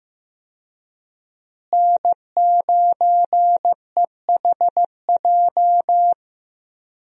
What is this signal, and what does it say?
Morse code, heard as a keyed tone, about 15 wpm, 700 Hz. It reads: N9EHJ
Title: USA Call Signs - 5 Characters - Repeated 2x - 15wpm